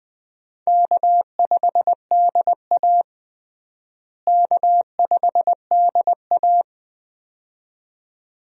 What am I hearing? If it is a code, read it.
Morse code, heard as a keyed tone, about 21 wpm, 700 Hz. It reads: K5DA K5DA
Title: USA Call Signs - 4 Characters - Repeated 2x - 20wpm